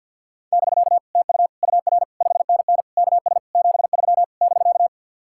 Morse code, heard as a keyed tone, about 63 wpm, 700 Hz. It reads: <BK> TU UR 5NN KS 73 <BK>